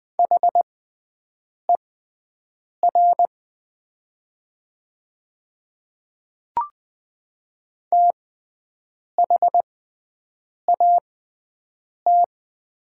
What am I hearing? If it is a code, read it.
Morse code, heard as a keyed tone, about 20 wpm, spaced as for 7 wpm, 700 Hz. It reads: HER THAT
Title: 6x Character Spacing - Top 100 Words - 20wpm